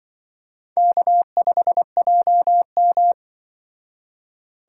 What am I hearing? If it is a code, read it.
Morse code, heard as a keyed tone, about 25 wpm, 700 Hz. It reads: K5JM
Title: Speed Racing - USA 1x2 Call Signs 20wpm